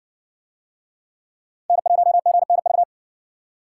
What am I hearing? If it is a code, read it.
Morse code, heard as a keyed tone, about 53 wpm, 700 Hz. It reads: N1GNV